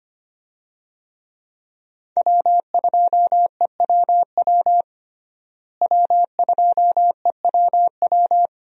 Morse code, heard as a keyed tone, about 25 wpm, 690 Hz. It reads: W2EWW W2EWW
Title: USA Call Signs - 5 Characters - Repeated 2x - 25wpm